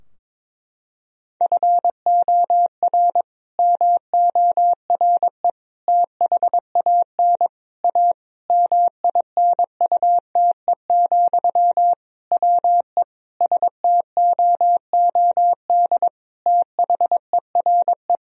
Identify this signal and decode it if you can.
Morse code, heard as a keyed tone, about 22 wpm, 700 Hz. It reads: FOR MORE THAN A MINUTE, WE STOOD THERE
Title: Sentences from Top 1000 Words - Review Part 2 - 22wpm